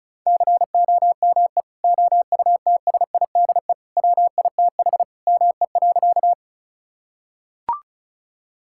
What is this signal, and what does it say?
Morse code, heard as a keyed tone, about 36 wpm, 700 Hz. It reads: COME OUTSIDE WITH ME.
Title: Sentences from Top 1000 Words - Review Part 6 - 35wpm